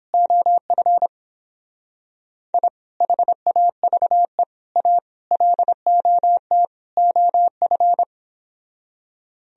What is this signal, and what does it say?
Morse code, heard as a keyed tone, about 31 wpm, 700 Hz. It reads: OF I HAVE A LOT OF